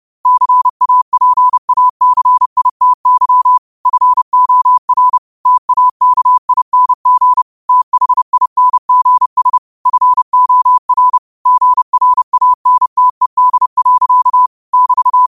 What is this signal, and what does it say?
Morse code, heard as a keyed tone, about 30 wpm, 1000 Hz. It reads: CAPACITY FOR TAKING THINGS FOR GRANTED. <BT>
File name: qotdfiles / 30wpm